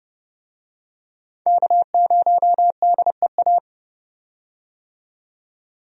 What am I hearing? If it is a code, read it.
Morse code, heard as a keyed tone, about 30 wpm, 700 Hz. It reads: K0DEA
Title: USA Call Signs - 5 Characters - 30wpm